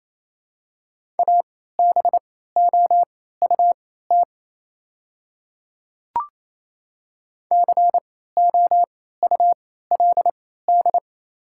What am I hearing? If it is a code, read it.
Morse code, heard as a keyed tone, about 28 wpm, spaced as for 16 wpm, 690 Hz. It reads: ABOUT COULD
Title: 3x Character Spacing - Top 100 Words - 28wpm